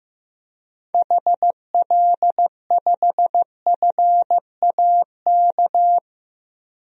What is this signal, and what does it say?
Morse code, heard as a keyed tone, about 15 wpm, 700 Hz. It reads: HL5FAK
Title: DXCC-100 Call Signs - 6 Characters - 15wpm